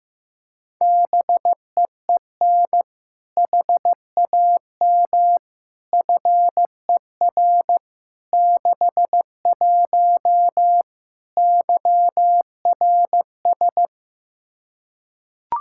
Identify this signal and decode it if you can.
Morse code, heard as a keyed tone, about 15 wpm, 700 Hz. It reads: BEEN HAM FER 61 YRS
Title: Practice QSO - Exchange 3 - More Info - 15wpm